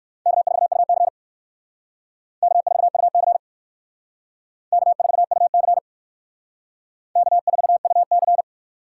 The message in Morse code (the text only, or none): K4UC K4UC K4UC K4UC